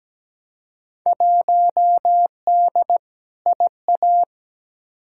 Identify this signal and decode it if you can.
Morse code, heard as a keyed tone, about 17 wpm, 700 Hz. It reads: 1D IA
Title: Field Day Exchanges - 17wpm